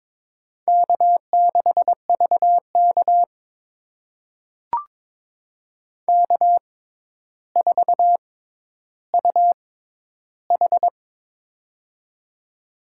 Morse code, heard as a keyed tone, about 22 wpm, spaced as for 11 wpm, 700 Hz. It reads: K6VK K4UH
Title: USA Call Signs - 4 Characters - 6x Character Spacing - 22wpm